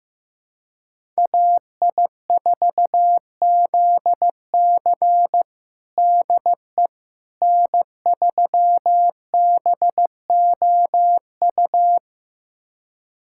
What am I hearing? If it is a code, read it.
Morse code, heard as a keyed tone, about 15 wpm, 700 Hz. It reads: AI4ZC DE N3BOU
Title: USA Callsign DE Callsign - 5 Characters - 15wpm